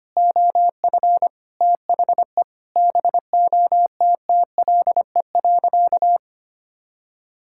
Morse code, heard as a keyed tone, about 25 wpm, 700 Hz. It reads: OF THE BOTTLE.
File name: practice / 25wpm / Sentences with 4-Word Phrases - 25wpm